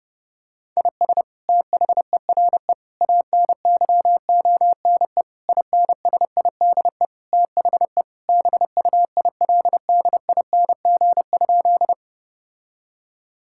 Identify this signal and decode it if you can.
Morse code, heard as a keyed tone, about 31 wpm, 700 Hz. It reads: IS THERE ANYONE INSIDE THE BUILDING?
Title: Sentences from Top 1000 Words - Review Part 6 - 30wpm